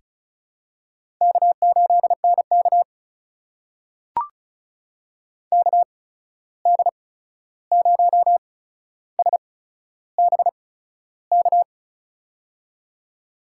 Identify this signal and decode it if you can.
Morse code, heard as a keyed tone, about 36 wpm, 700 Hz. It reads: K 8 N K K D 0 S B K
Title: USA Call Signs - 8x Character Spacing - 35wpm